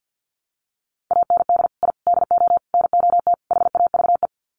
Morse code, heard as a keyed tone, about 48 wpm, 700 Hz. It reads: AND I DO NOT HAVE
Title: Sets of 5 Words Encore 50wpm